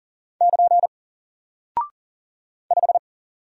Morse code, heard as a keyed tone, about 38 wpm, 700 Hz. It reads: <KN> 5
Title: Single Character (Letter:Number:Punctuation:Prosign) 40wpm